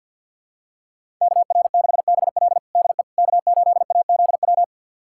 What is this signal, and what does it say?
Morse code, heard as a keyed tone, about 52 wpm, 700 Hz. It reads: KR6BL DE K8AZW
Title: USA Callsign DE Callsign - 5 Characters - 50wpm